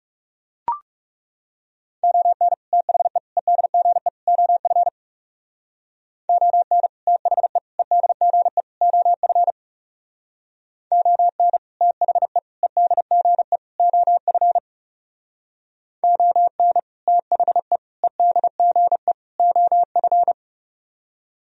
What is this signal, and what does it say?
Morse code, heard as a keyed tone, about 36 wpm, 700 Hz. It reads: ON THE EDGE OF ON THE EDGE OF ON THE EDGE OF ON THE EDGE OF
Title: Speed-Racing - Sets of 4 Words 30wpm